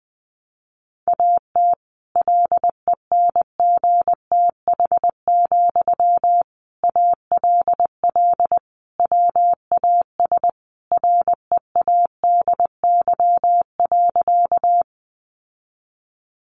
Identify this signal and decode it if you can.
Morse code, heard as a keyed tone, about 20 wpm, 700 Hz. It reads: AT LENGTH, ALL WAS READY.
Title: Sentences from Top 1000 Words 20wpm